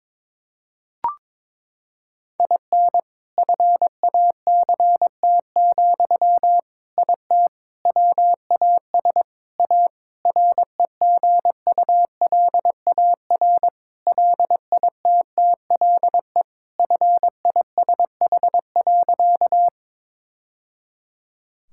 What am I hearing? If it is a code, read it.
Morse code, heard as a keyed tone, about 22 wpm, 700 Hz. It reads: IN FACT, IT WAS A REGULAR LITTLE FISH.
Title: Sentences from Top 1000 Words - Review Part 6 - 22wpm